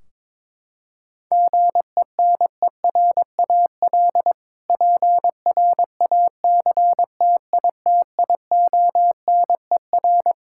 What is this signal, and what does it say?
Morse code, heard as a keyed tone, about 22 wpm, 700 Hz. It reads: GENERAL PRACTITIONER
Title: Practice QSO - Exchange 3 - More Info - 22wpm